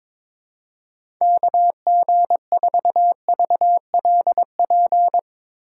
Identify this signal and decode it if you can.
Morse code, heard as a keyed tone, about 22 wpm, 700 Hz. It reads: KG4VLP
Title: USA Call Signs - 6 Characters - 2x Character Spacing - 22wpm